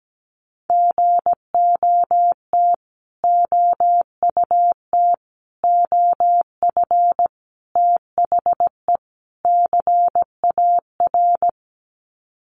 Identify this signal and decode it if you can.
Morse code, heard as a keyed tone, about 17 wpm, 700 Hz. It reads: GOT OUT OF THE CAR
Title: Sets of 5 Words Encore 17wpm